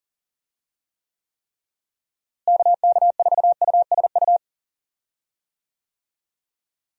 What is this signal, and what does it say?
Morse code, heard as a keyed tone, about 38 wpm, 700 Hz. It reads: KK4USU
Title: USA Call Signs - 6 Characters - Repeated 3x - 40wpm